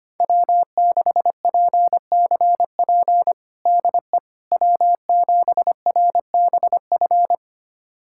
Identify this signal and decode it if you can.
Morse code, heard as a keyed tone, about 25 wpm, 700 Hz. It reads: W6PCP DE W7RBF
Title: USA Callsign DE Callsign - 5 Characters - 25wpm